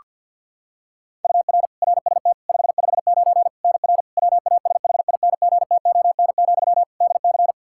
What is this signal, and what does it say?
Morse code, heard as a keyed tone, about 49 wpm, 700 Hz. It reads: UR RST 559 NR WASHINGTON, DC
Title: K1USN SST QSO Practice 50wpm